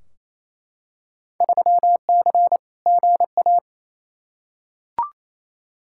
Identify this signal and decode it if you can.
Morse code, heard as a keyed tone, about 28 wpm, 700 Hz. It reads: 3C GA
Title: Field Day Exchanges Encore - 28wpm